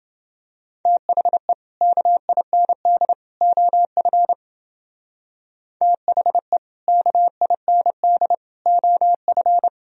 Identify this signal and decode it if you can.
Morse code, heard as a keyed tone, about 29 wpm, 700 Hz. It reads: THE KIND OF THE KIND OF